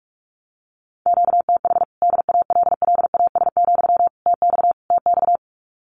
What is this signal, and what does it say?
Morse code, heard as a keyed tone, about 44 wpm, 700 Hz. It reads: QTH DALLAS, TX TX
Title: Common QSO Phrases Encore 45wpm